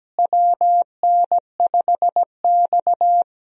Morse code, heard as a keyed tone, about 17 wpm, 700 Hz. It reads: WN5X
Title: USA Call Signs - 4 Characters - Repeated 3x - 17wpm